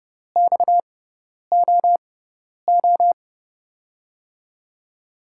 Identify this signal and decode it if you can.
Morse code, heard as a keyed tone, about 30 wpm, spaced as for 10 wpm, 700 Hz. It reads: XOO
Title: USA Call Signs - 5 Characters - 6x Character Spacing - 30wpm